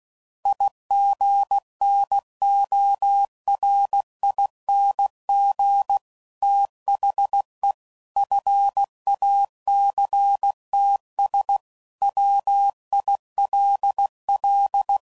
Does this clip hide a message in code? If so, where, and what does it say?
Yes: Morse code from 0.5 to 15.0 s, heard as a keyed tone, about 16 wpm, 800 Hz. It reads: IGNORING THE FACTS WILL